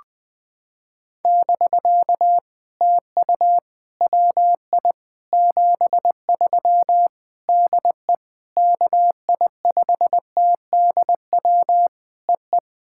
Morse code, heard as a keyed tone, about 20 wpm, 700 Hz. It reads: <BK> TU WI 73 DE KI5TDW EE
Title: POTA QSO Practice - 20wpm